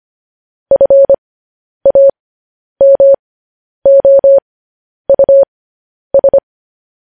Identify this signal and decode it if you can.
Morse code, heard as a keyed tone, about 25 wpm, spaced as for 10 wpm, 550 Hz. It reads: FAMOUS